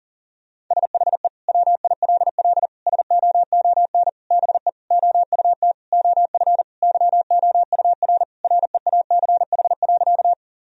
Morse code, heard as a keyed tone, about 40 wpm, 700 Hz. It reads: SHE WILL SOON BE OUT OF YOUR REACH.
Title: Sentences from Top 1000 Words - Review Part 2 - 40wpm